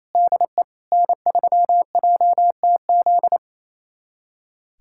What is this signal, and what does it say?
Morse code, heard as a keyed tone, about 28 wpm, 700 Hz. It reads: DE N3JTZ
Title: USA Callsign DE Callsign - 5 Characters - 28wpm